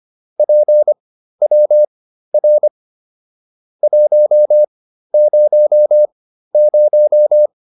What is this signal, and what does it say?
Morse code, heard as a keyed tone, about 25 wpm, spaced as for 13 wpm, 600 Hz. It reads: PWR 100